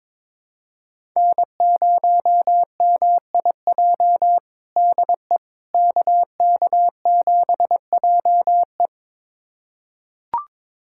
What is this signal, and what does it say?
Morse code, heard as a keyed tone, about 22 wpm, 700 Hz. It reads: N0MIJ DE KK7JE
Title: USA Callsign DE Callsign - 5 Characters - 22wpm